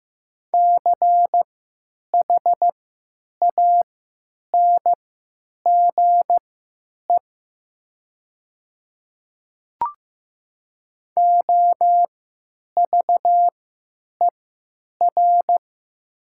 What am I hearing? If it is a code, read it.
Morse code, heard as a keyed tone, about 15 wpm, spaced as for 9 wpm, 700 Hz. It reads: CHANGE OVER
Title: 3x Character Spacing - Top 200 Words - 15wpm